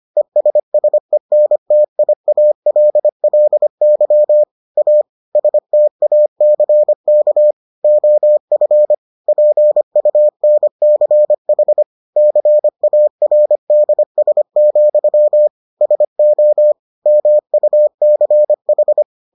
Morse code, heard as a keyed tone, about 25 wpm, 600 Hz. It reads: ESSENTIALLY A STACK OF PUNCH CARDS, SO MUCH